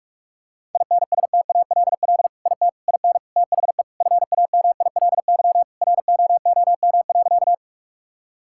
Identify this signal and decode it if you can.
Morse code, heard as a keyed tone, about 41 wpm, 700 Hz. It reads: INSTALL IT IN THE FAMILY ROOM.
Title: Sentences from Top 1600 Words 45wpm